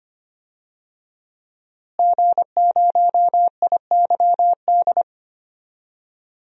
Morse code, heard as a keyed tone, about 25 wpm, 700 Hz. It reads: G0IYD
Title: DXCC-100 Call Signs - 5 Characters - Repeated 3x - 25wpm